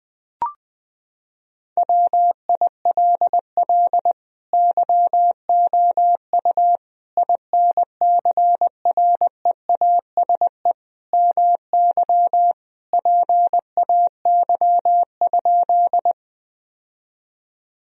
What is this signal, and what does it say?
Morse code, heard as a keyed tone, about 20 wpm, 700 Hz. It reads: WILL YOU INCREASE MY PAY?
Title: Sentences from Top 1000 Words - Review Part 4 - 20wpm